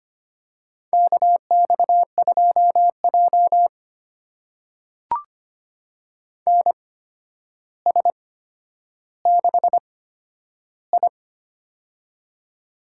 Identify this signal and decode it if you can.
Morse code, heard as a keyed tone, about 25 wpm, spaced as for 11 wpm, 700 Hz. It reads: KX2J NS6I
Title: USA Call Signs - 4 Characters - 8x Character Spacing - 25wpm